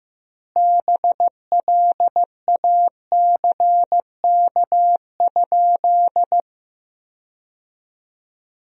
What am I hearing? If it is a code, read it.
Morse code, heard as a keyed tone, about 15 wpm, 700 Hz. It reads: BLACK?
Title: 2x Word-Spacing - Sentences from Top 400 Words - 15wpm